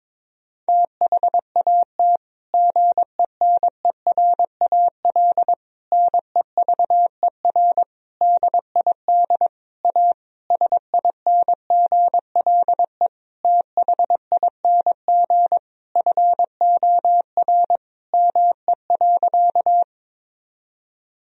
Morse code, heard as a keyed tone, about 22 wpm, 700 Hz. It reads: THAT GENERAL NEVER DID A SINGLE THING FOR ME.
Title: Sentences from Top 1000 Words - Review Part 4 - 22wpm